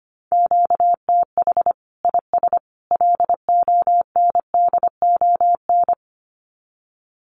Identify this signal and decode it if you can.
Morse code, heard as a keyed tone, about 25 wpm, 700 Hz. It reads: QTH IS LONDON
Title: Common QSO Phrases 25wpm